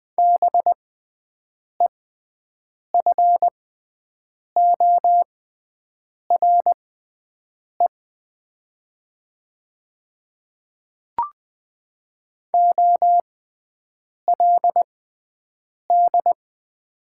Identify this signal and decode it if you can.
Morse code, heard as a keyed tone, about 20 wpm, spaced as for 7 wpm, 700 Hz. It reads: BEFORE OLD
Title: 6x Character Spacing - Top 200 Words - 20wpm